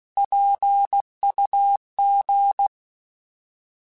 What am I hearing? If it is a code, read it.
Morse code, heard as a keyed tone, about 16 wpm, 800 Hz. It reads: PUG
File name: Alphabetical / Alphabetical mp3s / Ps @15